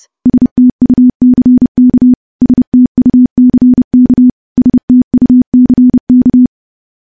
Morse code, heard as a keyed tone, about 30 wpm, 250 Hz. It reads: STUCK STUCK STUCK
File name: morse / morse-training-30WPM